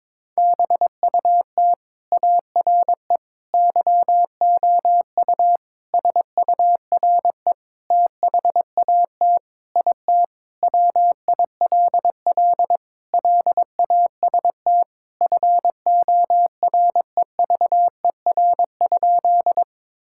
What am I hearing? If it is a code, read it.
Morse code, heard as a keyed tone, about 22 wpm, 700 Hz. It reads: BUT ARE YOU SURE THAT IT WILL LAST FOREVER?
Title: Sentences from Top 1600 Words 22wpm